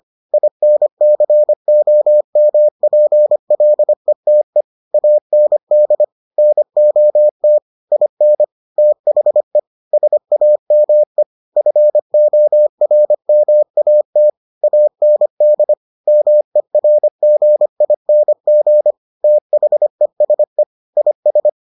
Morse code, heard as a keyed tone, about 25 wpm, 600 Hz. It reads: INCOMPLETE AND NOT IN THE SAME FORMAT AND MERGING THESE IS